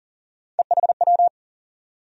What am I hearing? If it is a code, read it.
Morse code, heard as a keyed tone, about 40 wpm, 700 Hz. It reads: EHW
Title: USA Call Sign Postfixes - 3 Letters 40wpm